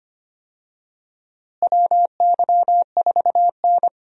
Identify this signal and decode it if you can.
Morse code, heard as a keyed tone, about 25 wpm, 700 Hz. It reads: WY4N
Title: USA Call Signs - 4 Characters - Repeated 2x - 25wpm